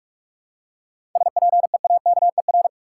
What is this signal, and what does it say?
Morse code, heard as a keyed tone, about 47 wpm, 700 Hz. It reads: SPEAKER
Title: Seven Letter Words - 45wpm